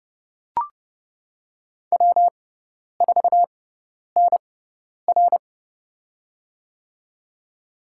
Morse code, heard as a keyed tone, about 30 wpm, spaced as for 10 wpm, 700 Hz. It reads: W4NR
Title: USA Call Signs - 4 Characters - 6x Character Spacing - 30wpm